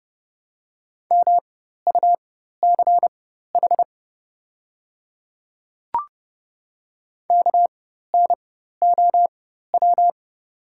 Morse code, heard as a keyed tone, about 30 wpm, spaced as for 14 wpm, 700 Hz. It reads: MUCH KNOW
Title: 4x Character Spacing - Top 200 Words - 30wpm